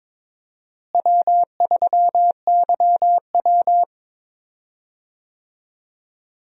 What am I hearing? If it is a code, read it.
Morse code, heard as a keyed tone, about 22 wpm, 700 Hz. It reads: W3YW
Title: USA Call Signs - 4 Characters - Repeated 3x - 22wpm